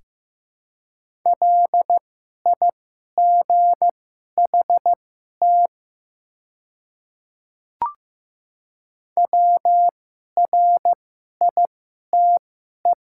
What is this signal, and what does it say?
Morse code, heard as a keyed tone, about 15 wpm, spaced as for 11 wpm, 700 Hz. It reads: LIGHT WRITE
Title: 2x Character Spacing - Top 200 Words - 15wpm